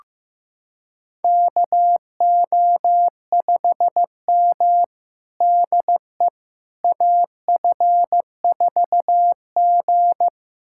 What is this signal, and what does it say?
Morse code, heard as a keyed tone, about 15 wpm, 700 Hz. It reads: KO5M DE AF4G
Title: USA Callsign DE Callsign - 4 Characters - 15wpm